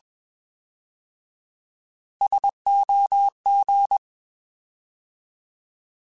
Morse code, heard as a keyed tone, about 21 wpm, 800 Hz. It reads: SOG
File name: Alphabetical / Alphabetical mp3s / Ss @20